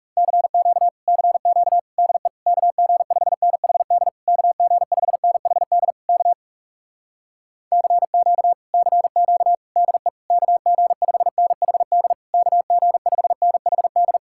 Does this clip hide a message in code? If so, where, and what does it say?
Yes: Morse code from 0.2 to 14.2 s, heard as a keyed tone, about 40 wpm, 700 Hz. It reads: CQ CQ DE KG5NHD KG5NHD K CQ CQ DE KG5NHD KG5NHD